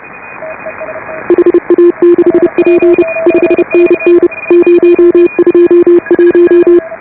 Morse code, heard as a keyed tone, about 34 wpm, 340 Hz. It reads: HA6P 5NN 021